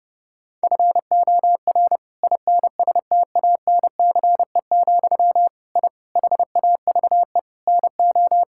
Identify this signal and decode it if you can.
Morse code, heard as a keyed tone, about 28 wpm, 700 Hz. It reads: FOR INSTANCE, I HAVE NO
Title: Sentences from Top 1000 Words - Review Part 5 - 30wpm